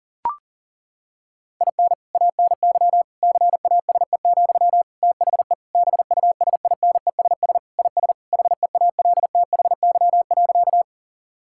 Morse code, heard as a keyed tone, about 40 wpm, 700 Hz. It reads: IN ANY CASE, THE BUSINESS IS HEALTHY.
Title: Sentences from Top 1000 Words - Review Part 5 - 40wpm